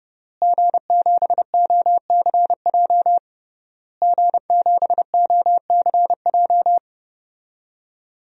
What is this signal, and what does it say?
Morse code, heard as a keyed tone, about 30 wpm, 700 Hz. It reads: G7OCJ G7OCJ
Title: DXCC-100 Call Signs - 5 Characters - Repeated 3x - 30wpm